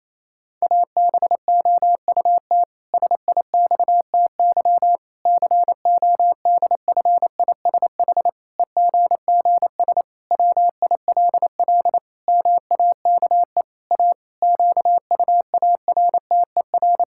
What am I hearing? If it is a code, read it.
Morse code, heard as a keyed tone, about 28 wpm, 700 Hz. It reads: ABOUT SIXTY CODFISH EGGS WILL MAKE A QUARTER
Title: Pangram Challenge v2 - 28wpm